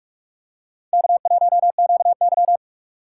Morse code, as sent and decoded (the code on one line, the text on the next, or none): -.- .---- --.- -.--
K1QY